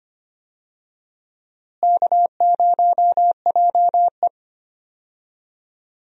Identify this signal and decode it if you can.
Morse code, heard as a keyed tone, about 25 wpm, 700 Hz. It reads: K0JE
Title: USA Call Signs - 4 Characters - Repeated 2x - 25wpm